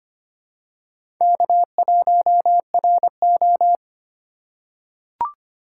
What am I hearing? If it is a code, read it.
Morse code, heard as a keyed tone, about 25 wpm, 700 Hz. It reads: K1RO
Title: K1USN SST QSO Practice - 25wpm